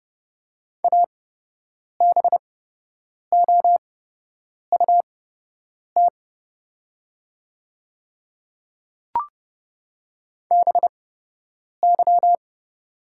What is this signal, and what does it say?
Morse code, heard as a keyed tone, about 30 wpm, spaced as for 8 wpm, 700 Hz. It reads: ABOUT BY
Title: 8x Character Spacing - Top 100 Words - 30wpm